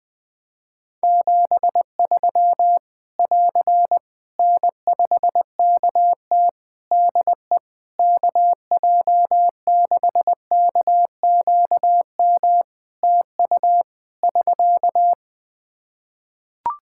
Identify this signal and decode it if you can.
Morse code, heard as a keyed tone, about 21 wpm, 700 Hz. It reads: 73 <AR> N5KT DE KJ6KQM TU <SK>
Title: Practice QSO - Exchange 4 - Ending - 20wpm